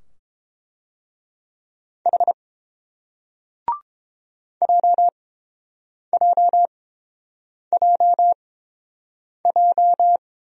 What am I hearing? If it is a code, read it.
Morse code, heard as a keyed tone, about 30 wpm, spaced as for 7 wpm, 700 Hz. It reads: H JJJJ